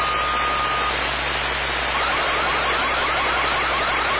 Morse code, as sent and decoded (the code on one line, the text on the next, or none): none